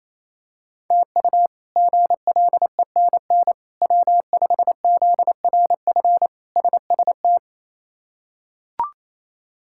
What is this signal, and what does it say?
Morse code, heard as a keyed tone, about 28 wpm, 700 Hz. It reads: TU GLENN W5ZRF SST
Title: K1USN SST QSO Practice - 28wpm